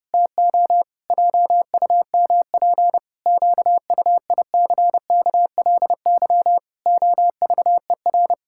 Morse code, heard as a keyed tone, about 30 wpm, 700 Hz. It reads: TO JUMP QUICKLY OVER
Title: Pangram Challenge v2 - 30wpm